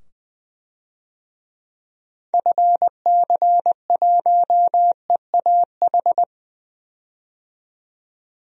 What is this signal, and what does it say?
Morse code, heard as a keyed tone, about 21 wpm, 700 Hz. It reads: FC1EAH
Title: DXCC-100 Call Signs - 6 Characters - 20wpm